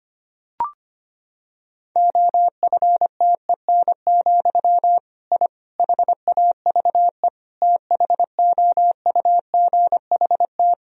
Morse code, heard as a keyed tone, about 25 wpm, 700 Hz. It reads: OFTEN, I HAVE THOUGHT
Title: Sentences from Top 1000 Words - Review Part 1 - 25wpm